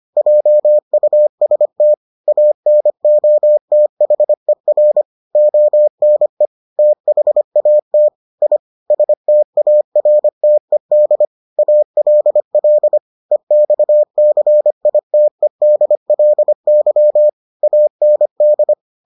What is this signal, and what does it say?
Morse code, heard as a keyed tone, about 25 wpm, 600 Hz. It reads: JUST ANOTHER ONE THAT I STARTED ALL EXCITEDLY AND